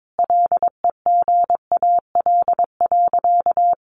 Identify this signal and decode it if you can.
Morse code, heard as a keyed tone, about 22 wpm, 700 Hz. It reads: LEGAL.
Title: Sentences from Top 1200 Words 22wpm